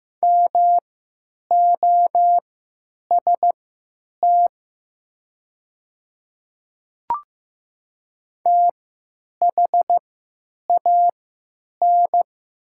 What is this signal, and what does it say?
Morse code, heard as a keyed tone, about 15 wpm, spaced as for 9 wpm, 690 Hz. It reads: MOST THAN
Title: 3x Character Spacing - Top 100 Words - 15wpm